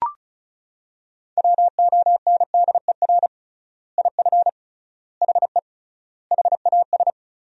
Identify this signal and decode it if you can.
Morse code, heard as a keyed tone, about 34 wpm, 700 Hz. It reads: WONDER IF HE HAS
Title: 3x Word-Spacing - Sentences from Top 500 Words - 35wpm